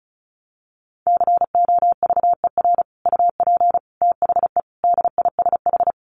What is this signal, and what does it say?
Morse code, heard as a keyed tone, about 34 wpm, 700 Hz. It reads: COVER UP THE DISH
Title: Sentences from Top 1300 Words 35wpm